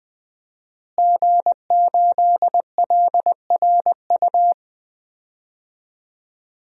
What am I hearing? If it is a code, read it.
Morse code, heard as a keyed tone, about 20 wpm, 700 Hz. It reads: G8LRU
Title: DXCC-100 Call Signs - 5 Characters - Repeated 3x - 20wpm